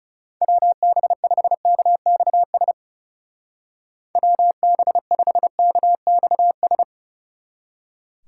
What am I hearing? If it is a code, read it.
Morse code, heard as a keyed tone, about 36 wpm, 700 Hz. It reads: WB5KXS WB5KXS
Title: Speed Racing - USA Call Signs 30wpm